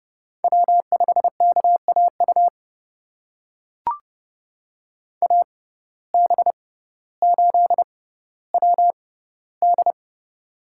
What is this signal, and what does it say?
Morse code, heard as a keyed tone, about 31 wpm, spaced as for 16 wpm, 700 Hz. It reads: W5KAU AB8WD
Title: USA Call Signs - 6x Character Spacing - 30wpm